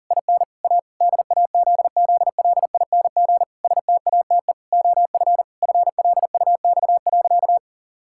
Morse code, heard as a keyed tone, about 40 wpm, 700 Hz. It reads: IN A DAZZLING STATE OF FLUX.
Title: Pangram Challenge v2 - 40wpm